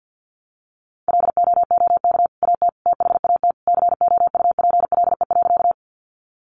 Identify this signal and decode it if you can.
Morse code, heard as a keyed tone, about 50 wpm, 700 Hz. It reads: LOOK AT THAT COUPLE.
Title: Sentences from Top 1100 Words 50wpm